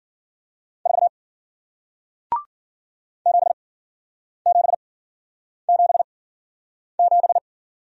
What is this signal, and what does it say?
Morse code, heard as a keyed tone, about 53 wpm, spaced as for 9 wpm, 700 Hz. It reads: 4 7777